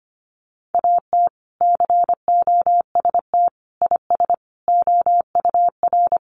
Morse code, heard as a keyed tone, about 25 wpm, 700 Hz. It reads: AT COST IS OUR
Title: Sentences from Top 1500 Words 25wpm